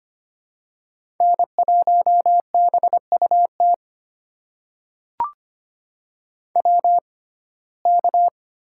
Morse code, heard as a keyed tone, about 25 wpm, spaced as for 18 wpm, 700 Hz. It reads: N1BUT WK
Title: USA Call Signs - 6x Character Spacing - 25wpm